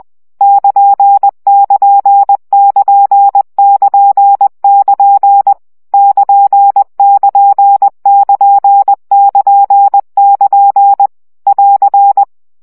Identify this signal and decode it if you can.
Morse code, heard as a keyed tone, about 20 wpm, 800 Hz. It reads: <KN><KN><KN><KN><KN> <KN><KN><KN><KN><KN> <AR>
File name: CW gyakorlas kn 17 baud